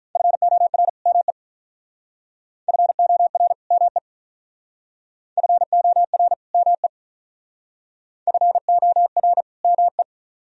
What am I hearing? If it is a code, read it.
Morse code, heard as a keyed tone, about 42 wpm, 700 Hz. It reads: FOR ME FOR ME FOR ME FOR ME
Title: Speed-Racing - Sets of 2 Words 35wpm